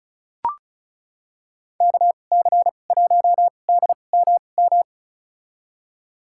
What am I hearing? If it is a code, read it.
Morse code, heard as a keyed tone, about 36 wpm, spaced as for 25 wpm, 700 Hz. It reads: KC1DMM
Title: USA Call Signs - 6 Characters - 2x Character Spacing - 35wpm